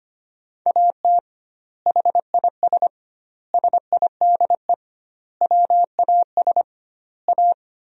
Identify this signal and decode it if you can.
Morse code, heard as a keyed tone, about 25 wpm, 700 Hz. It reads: AT HIS SIDE WAS A
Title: 2x Word-Spacing - Sentences from Top 500 Words - 25wpm